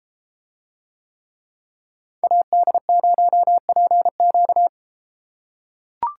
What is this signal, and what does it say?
Morse code, heard as a keyed tone, about 33 wpm, 700 Hz. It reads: AD0PQ